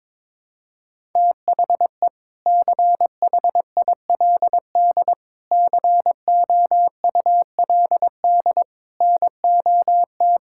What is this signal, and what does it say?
Morse code, heard as a keyed tone, about 22 wpm, 700 Hz. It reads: THE CHILD COULD NOT
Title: Sentences from Top 1000 Words - Review Part 6 - 22wpm